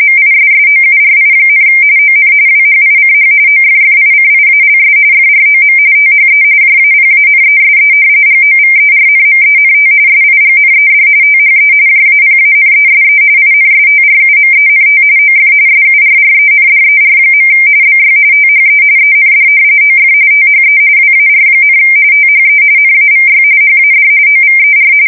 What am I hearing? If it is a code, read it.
RTTY, about 45.45 baud. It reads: MISSOURI, ON WHICH OFFICIALS FROM THE JAPANESE GOVERNMENT SIGNED THE JAPANESE INSTRUMENT OF SURRENDER. BY THE END OF THE WAR, THE NAVY HAD OVER
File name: fsk170 / usnavy1